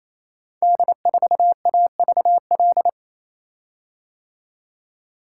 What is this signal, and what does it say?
Morse code, heard as a keyed tone, about 27 wpm, 700 Hz. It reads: D4AVL
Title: Code Groups - 5 Letter:Numbers - 28wpm